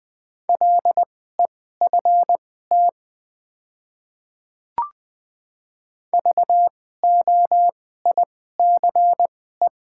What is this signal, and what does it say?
Morse code, heard as a keyed tone, about 20 wpm, spaced as for 14 wpm, 700 Hz. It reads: LEFT VOICE
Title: 2x Character Spacing - Top 500 Words - 20wpm